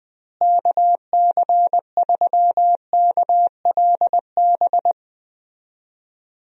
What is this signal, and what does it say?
Morse code, heard as a keyed tone, about 20 wpm, 700 Hz. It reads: KC3KLB
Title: USA Call Signs - 6 Characters - Repeated 3x - 20wpm